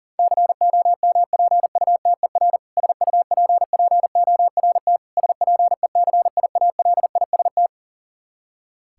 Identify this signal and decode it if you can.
Morse code, heard as a keyed tone, about 40 wpm, 700 Hz. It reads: COMPUTER SUPPORT SPECIALIST
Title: Practice QSO - Exchange 3 - More Info - 40wpm